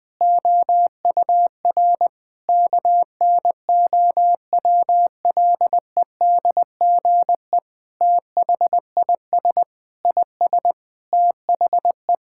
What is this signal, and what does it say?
Morse code, heard as a keyed tone, about 21 wpm, 700 Hz. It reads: OUR KNOWLEDGE THIS IS THE
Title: Sentences with 9-Word Phrases - 20wpm